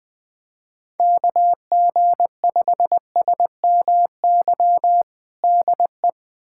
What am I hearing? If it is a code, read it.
Morse code, heard as a keyed tone, about 21 wpm, 700 Hz. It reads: KG5SMY DE
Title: USA Callsign DE Callsign - 6 Characters - 20wpm